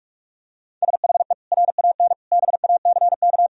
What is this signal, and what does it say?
Morse code, heard as a keyed tone, about 43 wpm, 700 Hz. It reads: SHE RAN BACK
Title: Sentences from Top 1000 Words - Review Part 3 - 45wpm